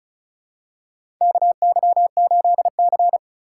Morse code, heard as a keyed tone, about 36 wpm, 700 Hz. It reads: KY8C
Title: USA Call Signs - 4 Characters - 2x Character Spacing - 35wpm